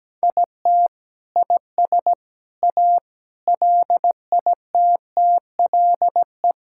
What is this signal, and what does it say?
Morse code, heard as a keyed tone, about 17 wpm, 700 Hz. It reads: IT IS A LITTLE
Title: Sentences from Top 1000 Words - Review Part 5 - 17wpm